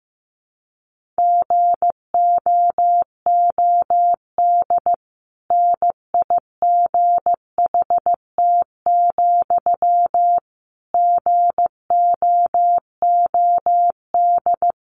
Morse code, heard as a keyed tone, about 15 wpm, 700 Hz. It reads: GOOD NIGHT, GOOD
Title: Sentences from Top 1500 Words 15wpm